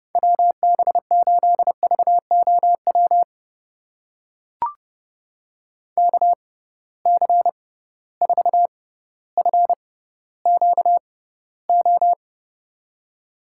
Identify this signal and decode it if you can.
Morse code, heard as a keyed tone, about 30 wpm, spaced as for 15 wpm, 700 Hz. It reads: WB8VOW KC4FQO
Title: USA Call Signs - 6 Characters - 6x Character Spacing - 30wpm